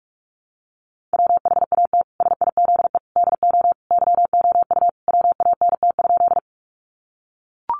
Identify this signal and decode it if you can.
Morse code, heard as a keyed tone, about 44 wpm, 700 Hz. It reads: WHAT SIZE DO YOU WANT?
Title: Sentences from Top 400 Words 45wpm